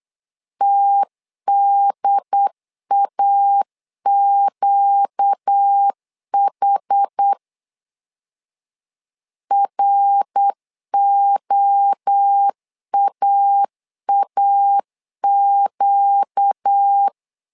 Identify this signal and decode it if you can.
Morse code, heard as a keyed tone, about 8 wpm, 800 Hz. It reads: TDAQH ROAAQ
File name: 4 Q 8wpm